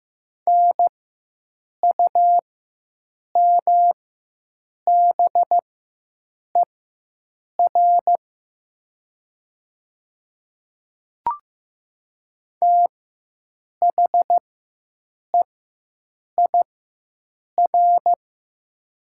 Morse code, heard as a keyed tone, about 15 wpm, spaced as for 7 wpm, 700 Hz. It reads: NUMBER THEIR
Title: 4x Character Spacing - Top 100 Words - 15wpm